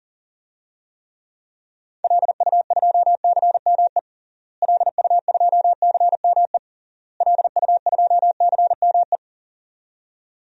Morse code, heard as a keyed tone, about 40 wpm, 700 Hz. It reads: LU2CME LU2CME LU2CME
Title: DXCC-100 Call Signs - 6 Characters - Repeated 3x - 40wpm